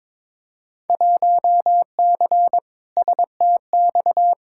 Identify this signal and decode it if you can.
Morse code, heard as a keyed tone, about 22 wpm, 700 Hz. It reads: 1C STX
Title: Field Day Exchanges Encore - 22wpm